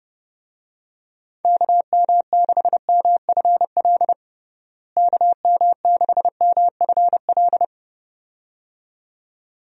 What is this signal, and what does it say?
Morse code, heard as a keyed tone, about 32 wpm, 700 Hz. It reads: KM6MFL KM6MFL
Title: USA Call Signs - 6 Characters - Repeated 2x - 30wpm